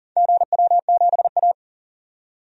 Morse code, heard as a keyed tone, about 40 wpm, 700 Hz. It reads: GW7A
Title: DXCC-100 Call Signs - 4 Characters - 40wpm